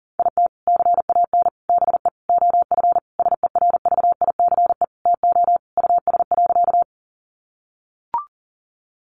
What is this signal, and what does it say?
Morse code, heard as a keyed tone, about 40 wpm, 700 Hz. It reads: IT CAN BE OF SERVICE TO US.
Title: Sentences from Top 1200 Words 40wpm